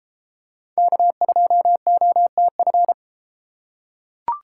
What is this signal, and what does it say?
Morse code, heard as a keyed tone, about 32 wpm, 700 Hz. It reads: K2OTF